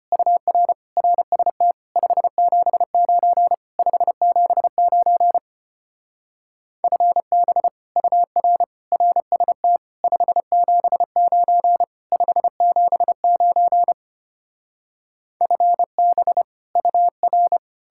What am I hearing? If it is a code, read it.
Morse code, heard as a keyed tone, about 33 wpm, 700 Hz. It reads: UR RST 579 579 FB UR RST 579 579 FB UR